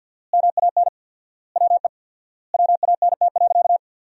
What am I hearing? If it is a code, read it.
Morse code, heard as a keyed tone, about 50 wpm, 700 Hz. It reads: MAN WE WANT.
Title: 4x Word-Spacing - Sentences from Top 200 Words - 50wpm